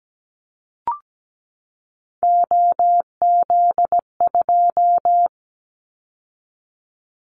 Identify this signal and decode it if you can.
Morse code, heard as a keyed tone, about 17 wpm, 700 Hz. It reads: OZ2
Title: Callsign Prefixes for Top 100 DXCC - 17wpm